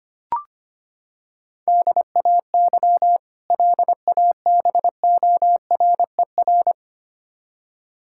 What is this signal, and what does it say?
Morse code, heard as a keyed tone, about 25 wpm, 700 Hz. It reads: DAY LABORER
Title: K1USN SST QSO Practice 25wpm